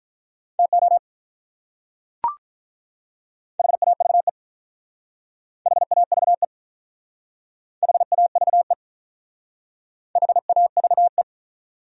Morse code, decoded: TO HAVE HAVE HAVE HAVE

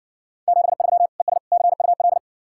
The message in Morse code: 73 ES CUL

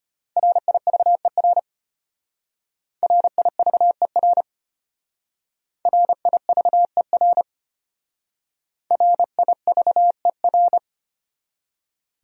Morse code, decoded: RIVER RIVER RIVER RIVER

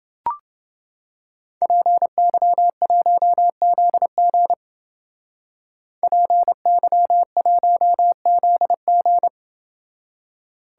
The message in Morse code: PY1ZG PY1ZG

